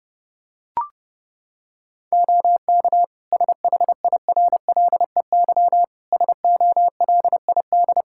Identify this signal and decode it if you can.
Morse code, heard as a keyed tone, about 30 wpm, 700 Hz. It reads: OK SHIRLEY SOLID
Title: Practice QSO - Exchange 3 - More Info - 30wpm